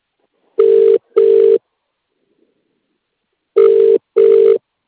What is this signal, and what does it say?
Ringback tone (UK), 400 + 450 Hz.